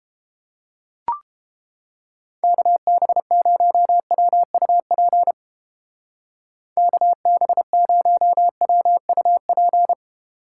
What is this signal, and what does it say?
Morse code, heard as a keyed tone, about 30 wpm, 700 Hz. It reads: KB0WUP KB0WUP